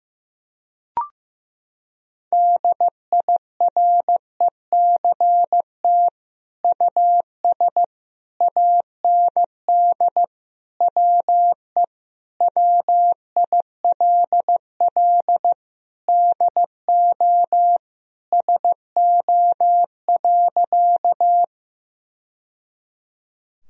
Morse code, heard as a keyed tone, about 15 wpm, 700 Hz. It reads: DIRECT US AND WE WILL DO SO.
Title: Sentences from Top 1000 Words - Review Part 2 - 15wpm